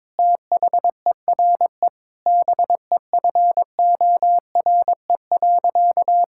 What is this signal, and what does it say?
Morse code, heard as a keyed tone, about 22 wpm, 700 Hz. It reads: THERE BEFORE.
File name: practice / 22wpm / Sentences with 3-Word Phrases - 22wpm